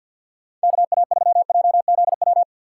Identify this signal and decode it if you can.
Morse code, heard as a keyed tone, about 47 wpm, 700 Hz. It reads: KA3JZW
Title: USA Call Signs - 2x Character Spacing - 50wpm